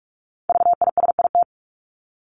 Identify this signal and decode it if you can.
Morse code, heard as a keyed tone, about 44 wpm, 700 Hz. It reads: VISIT
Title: Top 751-1000 Words 45wpm